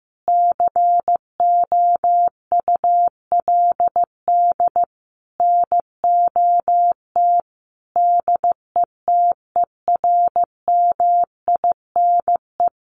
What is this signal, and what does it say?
Morse code, heard as a keyed tone, about 14 wpm, 700 Hz. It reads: COULD NOT DETERMINE
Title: Sentences from Top 700 Words 15wpm